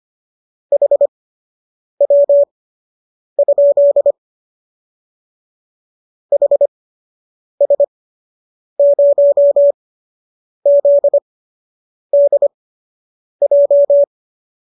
Morse code, heard as a keyed tone, about 25 wpm, spaced as for 8 wpm, 570 Hz. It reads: HW? HS0ZDJ